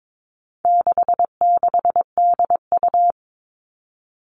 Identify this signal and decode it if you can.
Morse code, heard as a keyed tone, about 22 wpm, 700 Hz. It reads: 66DU
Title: All Cut Numbers - 4-Digit 22wpm